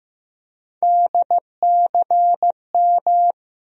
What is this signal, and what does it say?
Morse code, heard as a keyed tone, about 15 wpm, 700 Hz. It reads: DCM
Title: USA Call Sign Postfixes - 3 Letters 15wpm